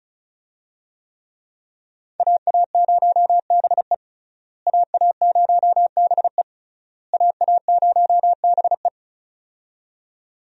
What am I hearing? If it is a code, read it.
Morse code, heard as a keyed tone, about 36 wpm, 700 Hz. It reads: AA0BE AA0BE AA0BE
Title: USA Call Signs - 5 Characters - Repeated 3x - 35wpm